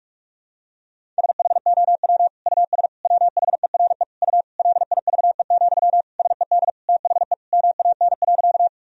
Morse code, heard as a keyed tone, about 46 wpm, 700 Hz. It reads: SHOW US WHERE U LIVE, SED THE MAN.
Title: Sentences from Top 200 Words and Abbreviations 45wpm